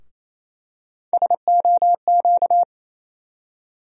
Morse code, heard as a keyed tone, about 28 wpm, 700 Hz. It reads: SOQ
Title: USA Call Sign Postfixes - 3 Letters 28wpm